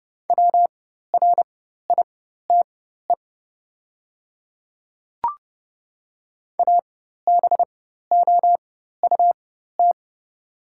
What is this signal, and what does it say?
Morse code, heard as a keyed tone, about 30 wpm, spaced as for 14 wpm, 700 Hz. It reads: WRITE ABOUT
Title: 4x Character Spacing - Top 100 Words - 30wpm